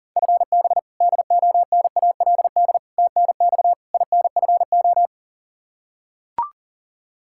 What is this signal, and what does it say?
Morse code, heard as a keyed tone, about 40 wpm, 700 Hz. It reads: FB DONALD TNX INFO
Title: Practice QSO - Exchange 3 - More Info - 40wpm